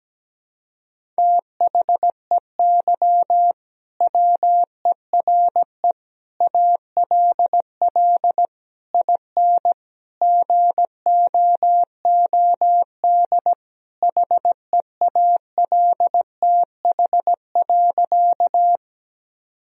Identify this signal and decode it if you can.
Morse code, heard as a keyed tone, about 17 wpm, 700 Hz. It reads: THEY WERE ALL IN GOOD HEALTH.
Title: Sentences from Top 1000 Words - Review Part 3 - 17wpm